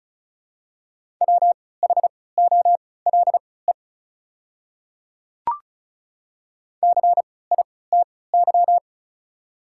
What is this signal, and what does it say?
Morse code, heard as a keyed tone, about 36 wpm, spaced as for 20 wpm, 700 Hz. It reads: WHOLE CITY
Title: 3x Character Spacing - Top 500 Words - 35wpm